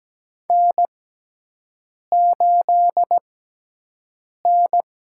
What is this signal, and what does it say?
Morse code, heard as a keyed tone, about 17 wpm, spaced as for 6 wpm, 700 Hz. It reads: N8N
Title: USA Call Signs - 5 Characters - 6x Character Spacing - 17wpm